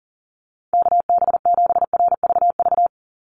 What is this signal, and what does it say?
Morse code, heard as a keyed tone, about 40 wpm, 700 Hz. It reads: KB7RVV
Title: USA Call Signs 40wpm